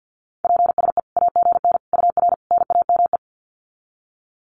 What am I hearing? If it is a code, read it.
Morse code, heard as a keyed tone, about 50 wpm, 700 Hz. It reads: PSE AGN UR NAME
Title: Common QSO Phrases 50wpm